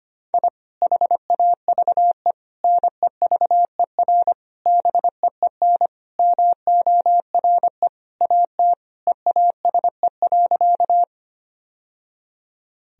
Morse code, heard as a keyed tone, about 25 wpm, 700 Hz. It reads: I HAVE NEVER BEEN MORE AT EASE.
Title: Sentences from Top 1000 Words - Review Part 1 - 25wpm